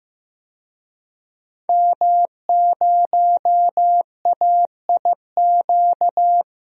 Morse code, heard as a keyed tone, about 15 wpm, 700 Hz. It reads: M0AIQ
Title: DXCC-100 Call Signs - 5 Characters - 15wpm